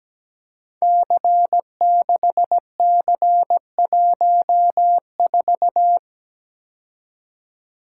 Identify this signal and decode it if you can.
Morse code, heard as a keyed tone, about 17 wpm, 700 Hz. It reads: C6C14
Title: Code Groups - 5 Letter:Numbers - 17wpm